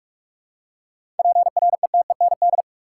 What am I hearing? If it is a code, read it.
Morse code, heard as a keyed tone, about 46 wpm, 700 Hz. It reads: PRETEND
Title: Seven Letter Words - 45wpm